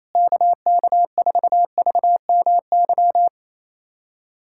Morse code, heard as a keyed tone, about 28 wpm, 700 Hz. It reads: KK4VMY
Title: USA Callsign DE Callsign 28wpm